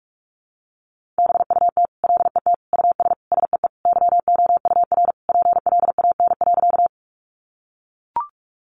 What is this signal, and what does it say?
Morse code, heard as a keyed tone, about 44 wpm, 700 Hz. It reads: BUT LET US SEE YOUR PLAN.
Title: Sentences from Top 500 Words 45wpm